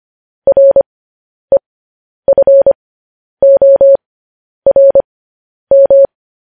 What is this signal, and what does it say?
Morse code, heard as a keyed tone, about 25 wpm, spaced as for 10 wpm, 550 Hz. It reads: REFORM
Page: news-headlines-in-morse-code-at-10-wpm